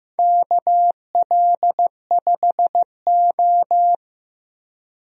Morse code, heard as a keyed tone, about 15 wpm, 700 Hz. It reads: KL5O
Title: USA Call Signs - 4 Characters - Repeated 3x - 15wpm